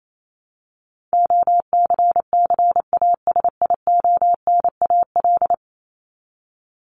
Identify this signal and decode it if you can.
Morse code, heard as a keyed tone, about 28 wpm, 700 Hz. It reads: OCCASIONAL
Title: LY Suffix - Top 100 Words with Base Words 28wpm